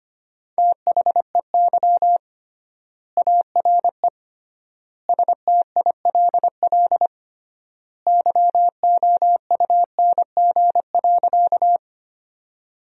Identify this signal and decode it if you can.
Morse code, heard as a keyed tone, about 25 wpm, 700 Hz. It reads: THEY ARE STILL YOUNG.
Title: 3x Word-Spacing - Sentences from Top 400 Words - 25wpm